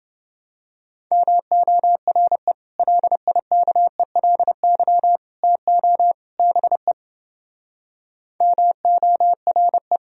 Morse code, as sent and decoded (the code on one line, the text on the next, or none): -- --- .-. / .-.. .. -.- . .-.. -.-- / - --- / -... / -- --- .-. .
MORE LIKELY TO BE MORE